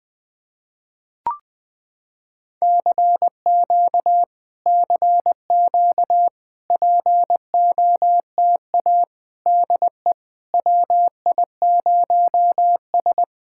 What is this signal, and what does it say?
Morse code, heard as a keyed tone, about 21 wpm, 700 Hz. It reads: CQ CQ POTA DE WI0S
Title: POTA QSO Practice - 20wpm